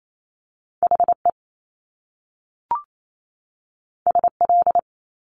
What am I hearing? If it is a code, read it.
Morse code, heard as a keyed tone, about 28 wpm, 700 Hz. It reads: HE SL
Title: Top Two Letter Combinations in Words 28wpm